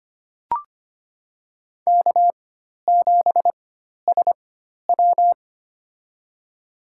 Morse code, heard as a keyed tone, about 25 wpm, spaced as for 12 wpm, 700 Hz. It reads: K7SW
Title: USA Call Signs - 4 Characters - 4x Character Spacing - 25wpm